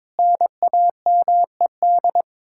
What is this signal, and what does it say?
Morse code, heard as a keyed tone, about 22 wpm, 700 Hz. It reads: NAMED
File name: practice / 22wpm / Sets of 6 Words - 22wpm